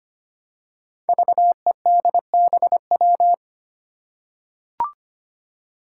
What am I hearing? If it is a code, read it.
Morse code, heard as a keyed tone, about 25 wpm, 700 Hz. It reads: VEDBW
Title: Five Challenging Letter-Number Combinations 25wpm